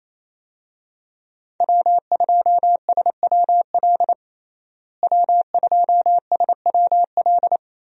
Morse code, heard as a keyed tone, about 28 wpm, 700 Hz. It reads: W2SWL W2SWL
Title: USA Call Signs - 5 Characters - Repeated 2x - 28wpm